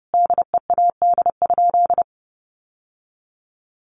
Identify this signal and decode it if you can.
Morse code, heard as a keyed tone, about 28 wpm, 700 Hz. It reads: DEAD?
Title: Sentences from Top 1500 Words 30wpm